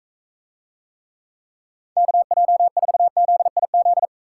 Morse code, heard as a keyed tone, about 42 wpm, 700 Hz. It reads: KJ4ZIZ